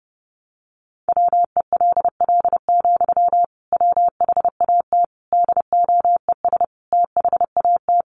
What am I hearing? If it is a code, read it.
Morse code, heard as a keyed tone, about 30 wpm, 700 Hz. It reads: WELL, WHAT DOES THAT
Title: Sentences from Top 500 Words 30wpm